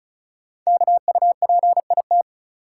Morse code, heard as a keyed tone, about 36 wpm, 700 Hz. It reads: KUPIT
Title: Code Groups - 5 Letter:Numbers - 35wpm